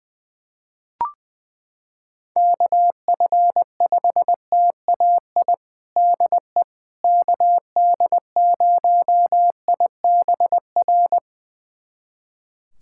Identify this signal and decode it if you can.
Morse code, heard as a keyed tone, about 20 wpm, 700 Hz. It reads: KF5TAI DE KD0IBR
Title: USA Callsign DE Callsign 20wpm